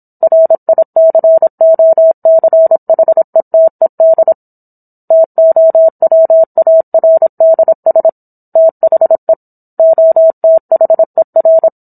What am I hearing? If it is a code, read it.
Morse code, heard as a keyed tone, about 27 wpm, 650 Hz. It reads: RICOCHETED TOWARDS THE OTHER